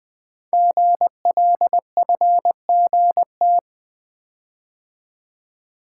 Morse code, heard as a keyed tone, about 20 wpm, 700 Hz. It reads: GLFGT
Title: Code Groups - 5 Letters - 20wpm